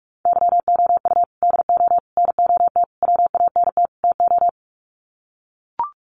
Morse code, heard as a keyed tone, about 44 wpm, 700 Hz. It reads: YOU DO NOT WANT TO
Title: Sets of 5 Words 45wpm